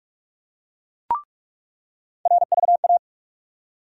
Morse code, heard as a keyed tone, about 46 wpm, 700 Hz. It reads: RVA
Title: USA Call Sign Postfixes - 3 Letters 45wpm